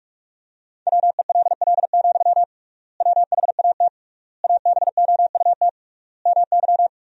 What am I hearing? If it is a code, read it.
Morse code, heard as a keyed tone, about 46 wpm, 700 Hz. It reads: WELL, WHAT ABOUT MY
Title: 3x Word-Spacing - Sentences from Top 500 Words - 45wpm